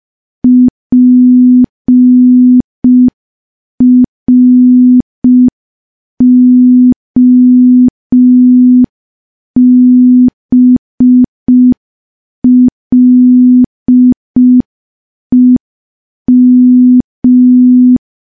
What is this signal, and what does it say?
Morse code, heard as a keyed tone, about 5 wpm, 250 Hz. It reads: PROBLEM
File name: morse / morse-training-05WPM